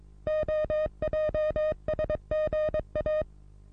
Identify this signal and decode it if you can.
Morse code, heard as a keyed tone, about 22 wpm, 620 Hz. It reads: OJSGA
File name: repos / Morsekurs / DJ4UF Morsen27